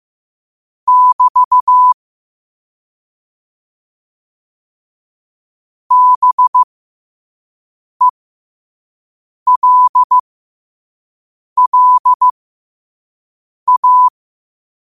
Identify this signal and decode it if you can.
Morse code, heard as a keyed tone, about 15 wpm, spaced as for 5 wpm, 1000 Hz. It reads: <BT> BELLA
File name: qotdfiles / 5wpm